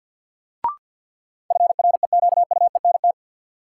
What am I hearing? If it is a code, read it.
Morse code, heard as a keyed tone, about 46 wpm, 700 Hz. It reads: FREQUENT
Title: Sentences from Top 1400 Words - New Word Drill - 50wpm